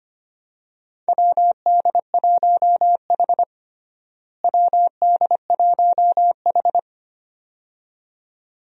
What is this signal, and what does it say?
Morse code, heard as a keyed tone, about 25 wpm, 700 Hz. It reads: WD1H WD1H
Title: USA Call Signs - 4 Characters - Repeated 2x - 25wpm